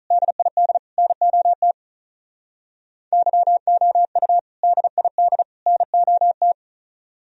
Morse code, heard as a keyed tone, about 45 wpm, 700 Hz. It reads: DID NOT YOU DID NOT